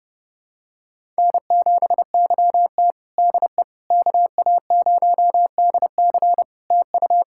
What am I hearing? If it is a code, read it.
Morse code, heard as a keyed tone, about 32 wpm, 700 Hz. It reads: N7YT DE KA0DC TU
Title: Practice QSO - Exchange 4 - Ending - 30wpm